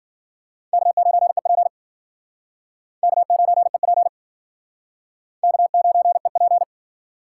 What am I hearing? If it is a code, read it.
Morse code, heard as a keyed tone, about 50 wpm, 700 Hz. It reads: K9EP K9EP K9EP
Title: Speed Racing - USA 1x2 Call Signs 40wpm